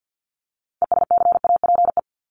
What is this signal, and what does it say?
Morse code, heard as a keyed tone, about 50 wpm, 700 Hz. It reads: ESCAPE
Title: Top 1501-1750 Words 50wpm